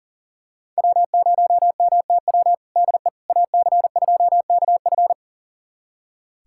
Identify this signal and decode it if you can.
Morse code, heard as a keyed tone, about 40 wpm, 700 Hz. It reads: W0MTW DE AC2KF
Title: USA Callsign DE Callsign - 5 Characters - 40wpm